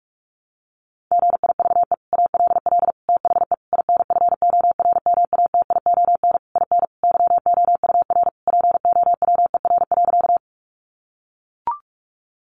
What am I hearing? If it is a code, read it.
Morse code, heard as a keyed tone, about 44 wpm, 700 Hz. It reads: GIVE ALL THE INFORMATION IN YOUR POWER.
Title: Sentences from Top 600 Words 45wpm